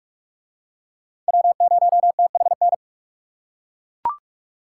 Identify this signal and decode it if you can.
Morse code, heard as a keyed tone, about 39 wpm, 700 Hz. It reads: W0THN